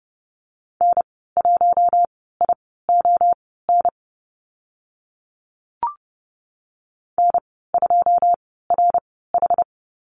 Morse code, heard as a keyed tone, about 30 wpm, spaced as for 17 wpm, 700 Hz. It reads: N1ION N2RH
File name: Lesson 14g - Number 2 - Call Signs 30wpm